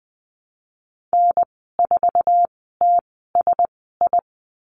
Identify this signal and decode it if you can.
Morse code, heard as a keyed tone, about 20 wpm, 700 Hz. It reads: N 4 T S I
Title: 055 Lesson 10g - Letter r - Call Signs 20wpm